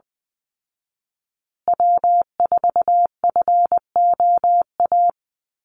Morse code, heard as a keyed tone, about 20 wpm, 700 Hz. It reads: W4FOA
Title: Call Signs Repeated 3x - 20wpm